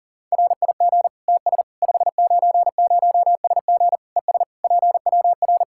Morse code, heard as a keyed tone, about 40 wpm, 700 Hz. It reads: RIG TS 590SG ES PWR
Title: K1USN SST QSO Practice 40wpm